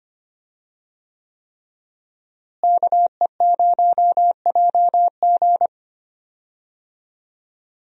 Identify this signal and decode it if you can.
Morse code, heard as a keyed tone, about 25 wpm, 700 Hz. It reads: KE0JG
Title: USA Call Signs - 5 Characters - Repeated 3x - 25wpm